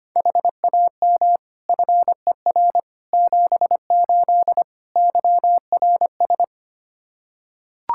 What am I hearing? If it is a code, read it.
Morse code, heard as a keyed tone, about 25 wpm, 700 Hz. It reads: HAM FER 78 YRS
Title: Practice QSO - Exchange 3 - More Info - 25wpm